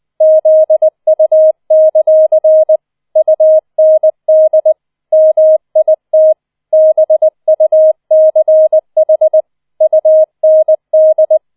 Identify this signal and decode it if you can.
Morse code, heard as a keyed tone, about 20 wpm, 630 Hz. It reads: ZU; UND MIT BUCH UND